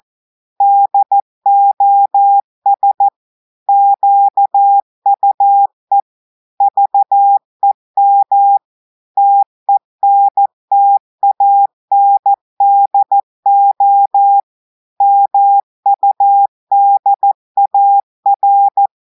Morse code, heard as a keyed tone, about 14 wpm, 800 Hz. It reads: DOS QUE VEM TENTANDO MUDAR
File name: audios / portugais / portugais texte en clair 1 vitesse 14 mots minutes